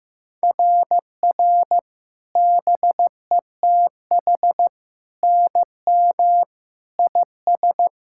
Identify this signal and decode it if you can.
Morse code, heard as a keyed tone, about 14 wpm, 700 Hz. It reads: RR BETH NM IS
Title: SOTA QSO Practice 15wpm